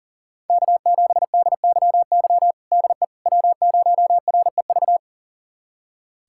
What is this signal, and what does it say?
Morse code, heard as a keyed tone, about 40 wpm, 700 Hz. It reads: K7DYY DE W0REV
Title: USA Callsign DE Callsign - 5 Characters - 40wpm